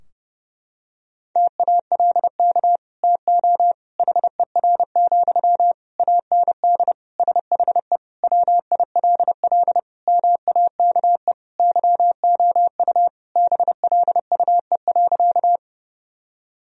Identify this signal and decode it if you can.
Morse code, heard as a keyed tone, about 30 wpm, 700 Hz. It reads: TALK TO HER, AND SHE WILL MAKE YOU BLUE.
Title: Sentences from Top 1000 Words - Review Part 3 - 30wpm